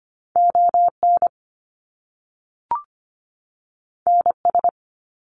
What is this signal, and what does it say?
Morse code, heard as a keyed tone, about 25 wpm, 700 Hz. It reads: ON NS
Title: Canadian Provinces and Territories 25wpm